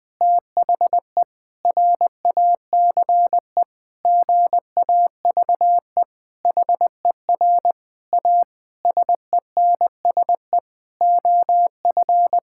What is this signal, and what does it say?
Morse code, heard as a keyed tone, about 21 wpm, 700 Hz. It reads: THE RACE GAVE HER A SENSE OF
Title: Difficult Sentences with Hints - 20wpm